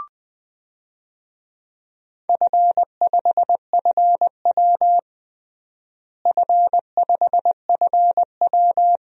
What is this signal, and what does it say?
Morse code, heard as a keyed tone, about 20 wpm, 700 Hz. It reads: F5FW F5FW
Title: DXCC-100 Call Signs - 4 Characters - Repeated 3x - 20wpm